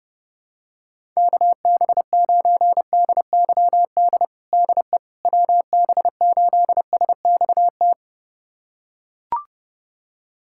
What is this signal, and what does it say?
Morse code, heard as a keyed tone, about 30 wpm, 700 Hz. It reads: KB9DYD DE WB8SXT
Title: USA Callsign DE Callsign - 6 Characters - 30wpm